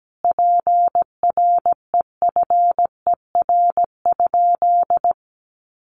Morse code, heard as a keyed tone, about 17 wpm, 700 Hz. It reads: PREFER?
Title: Sentences from Top 1100 Words 17wpm